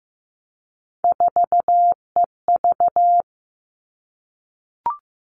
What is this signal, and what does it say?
Morse code, heard as a keyed tone, about 15 wpm, 700 Hz. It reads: 4EV
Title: Three Challenging Letter-Number Combinations 15wpm